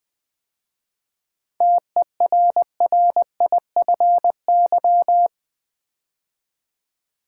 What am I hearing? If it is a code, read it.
Morse code, heard as a keyed tone, about 20 wpm, 700 Hz. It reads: TERRIFY
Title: 7 Letter Words - First 3 Letters Given - 20wpm